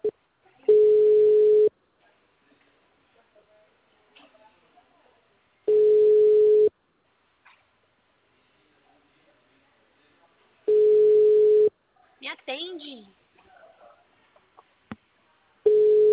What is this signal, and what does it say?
Ringback tone (Europe, 425 Hz), with speech.